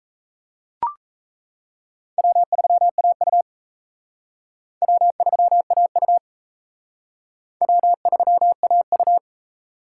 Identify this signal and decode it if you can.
Morse code, heard as a keyed tone, about 34 wpm, 700 Hz. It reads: W3AU W3AU W3AU
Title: Speed Racing - USA 1x2 Call Signs 28wpm